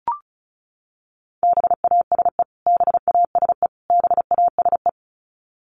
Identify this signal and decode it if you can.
Morse code, heard as a keyed tone, about 34 wpm, 700 Hz. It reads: BASE BASE BASE
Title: Top 500 Words Repeated 35wpm